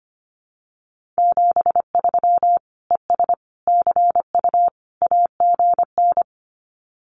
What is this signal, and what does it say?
Morse code, heard as a keyed tone, about 25 wpm, 700 Hz. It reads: 73 ES CU AGN
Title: Common QSO Phrases Encore 25wpm